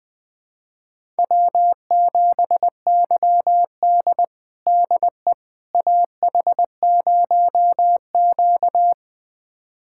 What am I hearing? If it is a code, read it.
Morse code, heard as a keyed tone, about 20 wpm, 700 Hz. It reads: W7YD DE AH0Q
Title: USA Callsign DE Callsign - 4 Characters - 20wpm